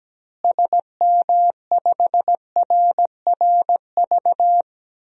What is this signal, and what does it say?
Morse code, heard as a keyed tone, about 17 wpm, 700 Hz. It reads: SM5RRV
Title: DXCC-100 Call Signs - 6 Characters - Repeated 3x - 17wpm